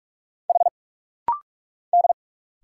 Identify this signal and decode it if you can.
Morse code, heard as a keyed tone, about 46 wpm, 700 Hz. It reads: H D
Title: Single Letter-Number Alt - Rapid-Fire 45wpm